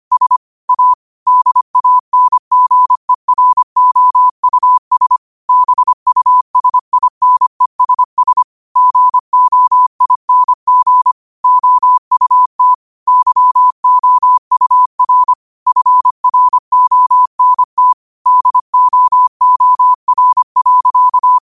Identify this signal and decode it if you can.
Morse code, heard as a keyed tone, about 25 wpm, 1000 Hz. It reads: S A DANGEROUS BUSINESS GOING OUT YOUR FRONT DOOR.
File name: qotdfiles / 25wpm